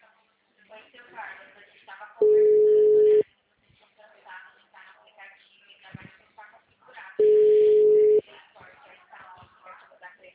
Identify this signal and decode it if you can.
Ringback tone (Europe, 425 Hz), with speech.